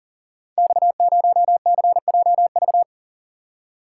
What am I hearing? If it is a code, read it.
Morse code, heard as a keyed tone, about 43 wpm, 700 Hz. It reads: X0CJV
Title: Code Groups - 5 Letter:Numbers - 40wpm